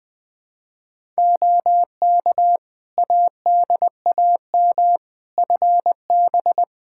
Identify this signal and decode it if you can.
Morse code, heard as a keyed tone, about 20 wpm, 700 Hz. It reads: OK ADAM FB
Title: Practice QSO - Exchange 2 - Info - 20wpm